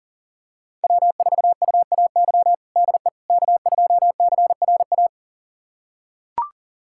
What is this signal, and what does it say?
Morse code, heard as a keyed tone, about 37 wpm, 700 Hz. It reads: W4UAY DE K2CRA
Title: USA Callsign DE Callsign - 5 Characters - 40wpm